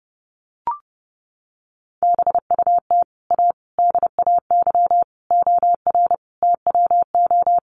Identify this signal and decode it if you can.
Morse code, heard as a keyed tone, about 30 wpm, 700 Hz. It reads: BUT A DAY OR TWO
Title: Sentences from Top 1100 Words 30wpm